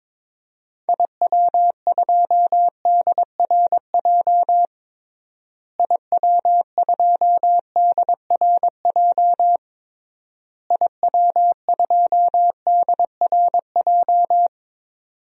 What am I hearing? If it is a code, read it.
Morse code, heard as a keyed tone, about 22 wpm, 700 Hz. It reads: IW2DRJ IW2DRJ IW2DRJ
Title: DXCC-100 Call Signs - 6 Characters - Repeated 3x - 22wpm